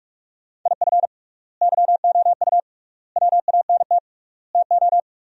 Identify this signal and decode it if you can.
Morse code, heard as a keyed tone, about 47 wpm, 700 Hz. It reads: IF YOU WANT TO
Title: 3x Word-Spacing - Sentences from Top 200 Words - 45wpm